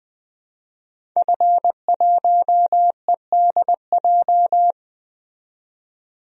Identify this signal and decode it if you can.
Morse code, heard as a keyed tone, about 20 wpm, 700 Hz. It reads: F1EDJ
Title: DXCC-100 Call Signs - 5 Characters - 20wpm